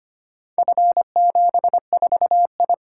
Morse code, heard as a keyed tone, about 25 wpm, 700 Hz. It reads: F74I
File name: practice / 25wpm / Four Letter-Number Combinations - 25wpm